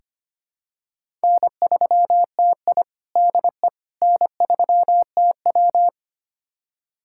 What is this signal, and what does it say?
Morse code, heard as a keyed tone, about 25 wpm, 700 Hz. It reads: N3TI DE N3TW
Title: USA Callsign DE Callsign - 4 Characters - 25wpm